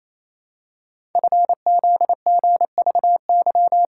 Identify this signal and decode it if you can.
Morse code, heard as a keyed tone, about 27 wpm, 700 Hz. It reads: FZGVY
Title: Code Groups - 5 Letters - 28wpm